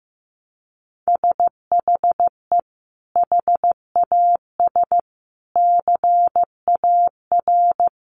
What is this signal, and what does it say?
Morse code, heard as a keyed tone, about 15 wpm, 700 Hz. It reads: SHE HAS CAR
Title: Sentences from Top 1100 Words 15wpm